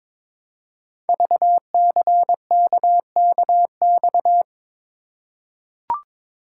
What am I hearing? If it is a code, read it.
Morse code, heard as a keyed tone, about 22 wpm, 700 Hz. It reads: VCKKX
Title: Code Groups - 5 Letters - 22wpm